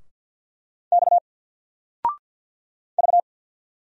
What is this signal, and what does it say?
Morse code, heard as a keyed tone, about 48 wpm, 700 Hz. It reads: X V
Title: Single Letter-Number Alt 50wpm